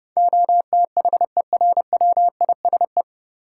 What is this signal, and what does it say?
Morse code, heard as a keyed tone, about 31 wpm, 700 Hz. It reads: OTHERWISE
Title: Sentences from Top 1200 Words - New Word Drill - 30wpm